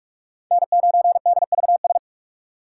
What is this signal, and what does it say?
Morse code, heard as a keyed tone, about 43 wpm, 700 Hz. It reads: N9DVS
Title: USA Call Signs - 5 Characters - 4x Character Spacing - 45wpm